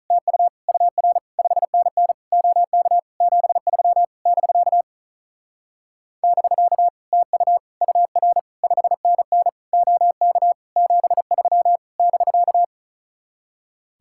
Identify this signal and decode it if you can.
Morse code, heard as a keyed tone, about 43 wpm, 700 Hz. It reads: TU UR 5NN OK 73 <BK> <BK> TU UR 5NN OK 73 <BK>
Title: Speed-Racing Common QSO Phrases Encore - 35wpm